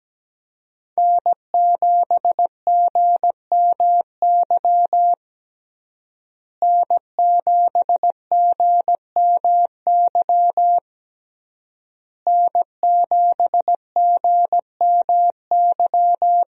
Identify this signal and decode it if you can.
Morse code, heard as a keyed tone, about 17 wpm, 700 Hz. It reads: N7GMY N7GMY N7GMY
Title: USA Call Signs - 5 Characters - Repeated 3x - 17wpm